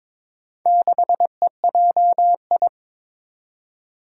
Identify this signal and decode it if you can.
Morse code, heard as a keyed tone, about 22 wpm, 700 Hz. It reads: 6EJI
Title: Four Challenging Letter-Number Combinations 22wpm